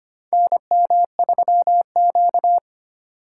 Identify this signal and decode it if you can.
Morse code, heard as a keyed tone, about 25 wpm, 700 Hz. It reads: NM3Q
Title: USA Call Signs - 4 Characters - 2x Character Spacing - 25wpm